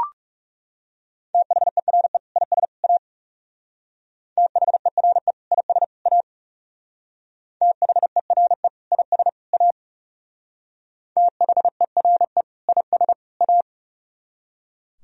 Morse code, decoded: THERE IS A THERE IS A THERE IS A THERE IS A

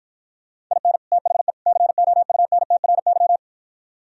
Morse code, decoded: IN THE COUNTRY

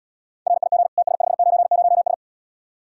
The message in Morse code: RIG TS590S

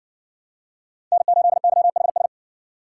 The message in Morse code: N8YHS

